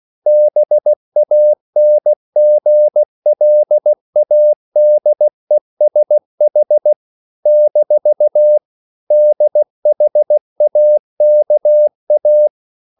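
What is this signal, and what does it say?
Morse code, heard as a keyed tone, about 16 wpm, 600 Hz. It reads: BANGLADESH - DHAKA
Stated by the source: CW practice of national capitals at 16 words per minute, from ON6ZQ.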